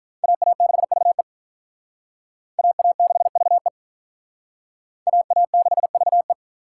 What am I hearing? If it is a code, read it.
Morse code, heard as a keyed tone, about 46 wpm, 700 Hz. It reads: AA6VE AA6VE AA6VE